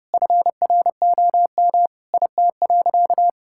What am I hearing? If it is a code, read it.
Morse code, heard as a keyed tone, about 30 wpm, 700 Hz. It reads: FROM IT.
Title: Sentences from Top 1000 Words - Review Part 5 - 30wpm